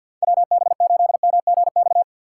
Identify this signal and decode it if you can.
Morse code, heard as a keyed tone, about 47 wpm, 700 Hz. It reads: WB8MGX
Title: USA Call Signs - 6 Characters - 3x Character Spacing - 50wpm